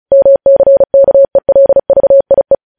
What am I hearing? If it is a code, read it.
Morse code, heard as a keyed tone, about 36 wpm, 550 Hz. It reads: MCKELVIE
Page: News headlines in Morse code at 35 wpm